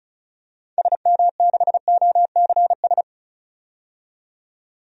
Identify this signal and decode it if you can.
Morse code, heard as a keyed tone, about 36 wpm, 700 Hz. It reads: SM6OCS
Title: DXCC-100 Call Signs - 6 Characters - 35wpm